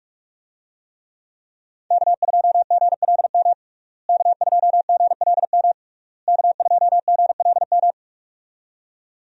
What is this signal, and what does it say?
Morse code, heard as a keyed tone, about 42 wpm, 700 Hz. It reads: K2GLM K2GLM K2GLM
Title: USA Call Signs - 5 Characters - Repeated 3x - 45wpm